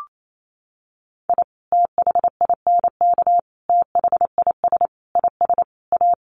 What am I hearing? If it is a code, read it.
Morse code, heard as a keyed tone, about 28 wpm, 700 Hz. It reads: I THINK THIS IS A
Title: Sets of 5 Words 28wpm